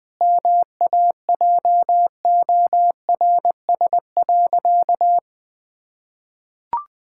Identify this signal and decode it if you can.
Morse code, heard as a keyed tone, about 21 wpm, 700 Hz. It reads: MAJORS.
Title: Pangram Challenge v2 - 20wpm